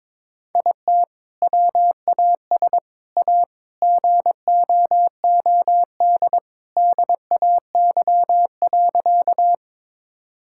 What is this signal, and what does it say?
Morse code, heard as a keyed tone, about 21 wpm, 700 Hz. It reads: IT WAS A GOOD DAY.
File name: practice / 22wpm / Sentences with 2-Word Phrases - 22wpm